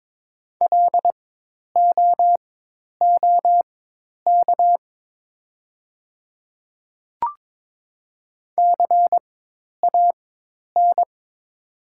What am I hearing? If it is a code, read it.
Morse code, heard as a keyed tone, about 22 wpm, spaced as for 10 wpm, 700 Hz. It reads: LOOK CAN
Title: 4x Character Spacing - Top 200 Words - 22wpm